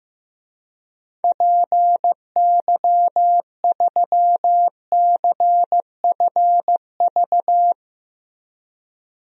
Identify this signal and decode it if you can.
Morse code, heard as a keyed tone, about 15 wpm, 700 Hz. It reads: PY3CFV
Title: DXCC-100 Call Signs - 6 Characters - Repeated 2x - 15wpm